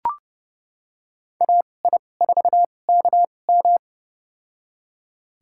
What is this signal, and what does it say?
Morse code, heard as a keyed tone, about 30 wpm, spaced as for 22 wpm, 700 Hz. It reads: AI4KM
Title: USA Call Signs - 2x Character Spacing - 30wpm